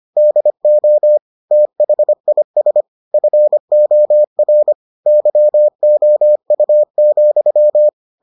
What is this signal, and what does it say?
Morse code, heard as a keyed tone, about 25 wpm, 600 Hz. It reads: DO THIS FOR YOU,